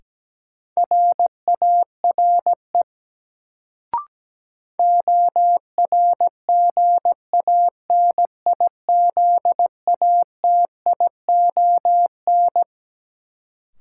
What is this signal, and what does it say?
Morse code, heard as a keyed tone, about 17 wpm, 700 Hz. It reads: RARE ORGANIZATION
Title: Sentences from Top 1100 Words - New Word Drill Alt - 17wpm